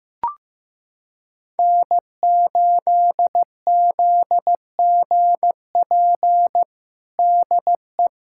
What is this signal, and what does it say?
Morse code, heard as a keyed tone, about 15 wpm, 700 Hz. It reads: N8ZGP DE
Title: USA Callsign DE Callsign - 5 Characters - 15wpm